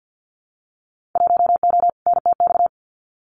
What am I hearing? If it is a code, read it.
Morse code, heard as a keyed tone, about 50 wpm, 700 Hz. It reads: 1O NTX
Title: Winter Field Day Exchanges 50wpm